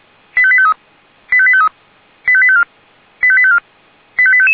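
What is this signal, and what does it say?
Five-tone selective call (ZVEI), heard as quick runs of five tones, tones of about 70 ms. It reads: 85853 85853 85854 85854 85855
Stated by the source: ZVEI five-tone selective call, code 85741, received with an SDR.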